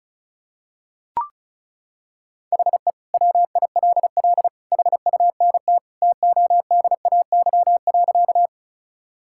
Morse code, heard as a keyed tone, about 36 wpm, 700 Hz. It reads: HE WILL HUNT TODAY.
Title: Sentences from Top 1600 Words 35wpm